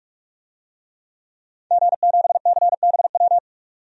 Morse code, heard as a keyed tone, about 45 wpm, 700 Hz. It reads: G7CBW
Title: DXCC-100 Call Signs - 5 Characters - 45wpm